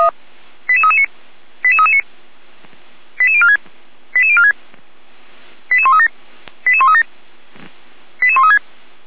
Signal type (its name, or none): five-tone selective call